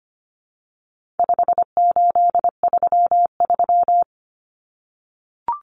Morse code, heard as a keyed tone, about 25 wpm, 700 Hz. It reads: 5833
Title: Numbers - 4-Digit 25wpm